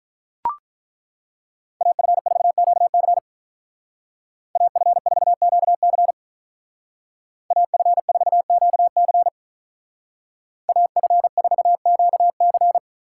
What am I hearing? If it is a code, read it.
Morse code, heard as a keyed tone, about 41 wpm, 700 Hz. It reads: AF4QC AF4QC AF4QC AF4QC
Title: Speed Racing - USA 2x2 Call Signs 35wpm